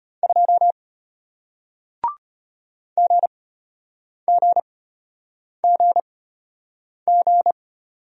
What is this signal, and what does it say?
Morse code, heard as a keyed tone, about 33 wpm, spaced as for 8 wpm, 700 Hz. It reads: 2 GGGG